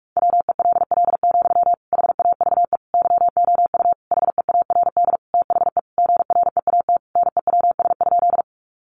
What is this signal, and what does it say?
Morse code, heard as a keyed tone, about 44 wpm, 700 Hz. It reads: WELL, HAVE YOU HEARD THE GREAT NEWS?
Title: Sentences from Top 800 Words 45wpm